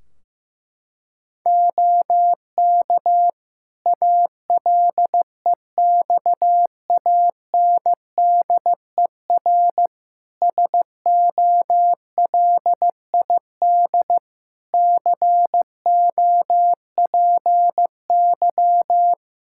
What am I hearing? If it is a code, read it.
Morse code, heard as a keyed tone, about 15 wpm, 700 Hz. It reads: OK ALEXANDER SOLID COPY
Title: K1USN SST QSO Practice 15wpm